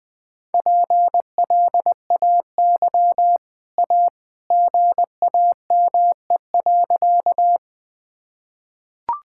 Morse code, heard as a keyed tone, about 19 wpm, 700 Hz. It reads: PLAY A GAME.
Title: Easy Sentences with Hints - 20wpm